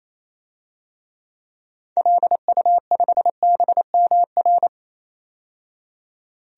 Morse code, heard as a keyed tone, about 28 wpm, 700 Hz. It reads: LU5BMR
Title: DXCC-100 Call Signs - 6 Characters - 28wpm